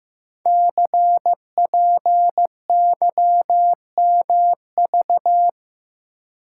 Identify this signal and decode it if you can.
Morse code, heard as a keyed tone, about 15 wpm, 700 Hz. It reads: CPYMV
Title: Code Groups - 5 Letter:Numbers - 15wpm